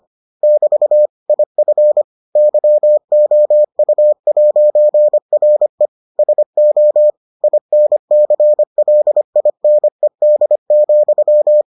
Morse code, heard as a keyed tone, about 25 wpm, 600 Hz. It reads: <BT> IF YOU'RE SO INCLINED,